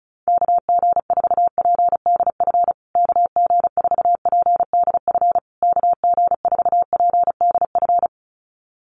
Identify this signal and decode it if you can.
Morse code, heard as a keyed tone, about 34 wpm, 700 Hz. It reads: KG4PDF KG4PDF KG4PDF
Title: USA Call Signs Repeated 3x 35wpm